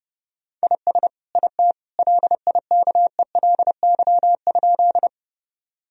Morse code, heard as a keyed tone, about 32 wpm, 700 Hz. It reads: IS IT LIKELY?
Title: Sentences from Top 1000 Words - Review Part 4 - 30wpm